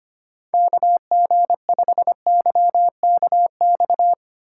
Morse code, heard as a keyed tone, about 32 wpm, 700 Hz. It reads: KG5YKX